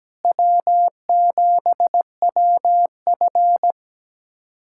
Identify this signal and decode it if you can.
Morse code, heard as a keyed tone, about 17 wpm, 700 Hz. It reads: W7WF
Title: USA Callsign DE Callsign - 4 Characters - 17wpm